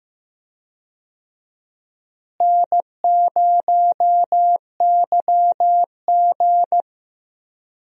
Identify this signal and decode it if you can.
Morse code, heard as a keyed tone, about 15 wpm, 700 Hz. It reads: N0YG
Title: USA Call Signs - 4 Characters - Repeated 2x - 15wpm